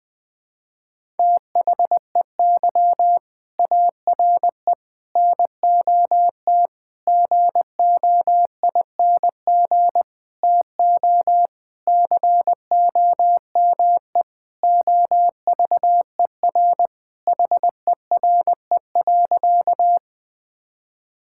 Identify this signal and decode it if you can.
Morse code, heard as a keyed tone, about 20 wpm, 700 Hz. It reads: THEY ARE NOT GOING TO COME OVER HERE.
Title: Sentences with 4-Word Phrases - Spoken Beginning - 20wpm